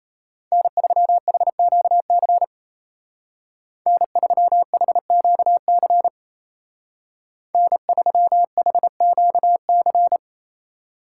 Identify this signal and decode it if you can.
Morse code, heard as a keyed tone, about 34 wpm, 700 Hz. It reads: N3HQC N3HQC N3HQC
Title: Speed Racing - USA 1x3 Call Signs 28wpm